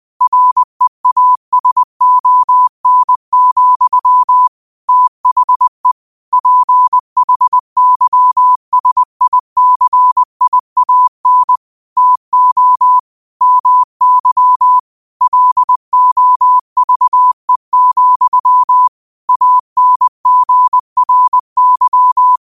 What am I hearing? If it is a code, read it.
Morse code, heard as a keyed tone, about 20 wpm, 1000 Hz. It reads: REASON, THE PHYSICIAN TO MY LOVE, ANGRY
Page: New quotes every day in morse code at 20 Words per minute.